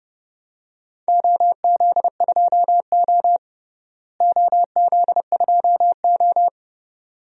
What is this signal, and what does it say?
Morse code, heard as a keyed tone, about 30 wpm, 700 Hz. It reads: OZ2O OZ2O
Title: DXCC-100 Call Signs - 4 Characters - Repeated 2x - 30wpm